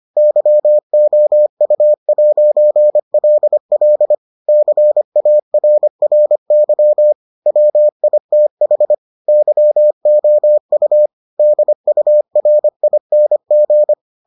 Morse code, heard as a keyed tone, about 25 wpm, 600 Hz. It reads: YOU'LL CARRY WITH YOU DURING